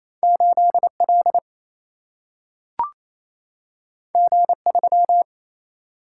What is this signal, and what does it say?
Morse code, heard as a keyed tone, about 28 wpm, 700 Hz. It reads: 8 L G 3
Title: Single-Letter Single-Number 28wpm